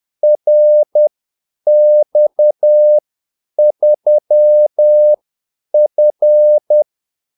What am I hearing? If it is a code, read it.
Morse code, heard as a keyed tone, about 10 wpm, spaced as for 8 wpm, 600 Hz. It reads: RX3F
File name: Callsigns-10-8wpm